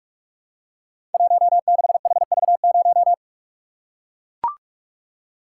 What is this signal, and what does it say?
Morse code, heard as a keyed tone, about 44 wpm, 700 Hz. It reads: 16HV0
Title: Code Groups - 5 Letter:Numbers - 45wpm